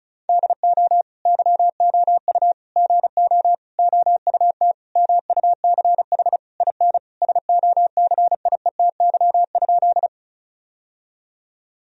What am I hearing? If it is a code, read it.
Morse code, heard as a keyed tone, about 36 wpm, 700 Hz. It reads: DO YOU GO OUT MUCH IN SOCIETY?
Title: Sentences from Top 1000 Words - Review Part 4 - 35wpm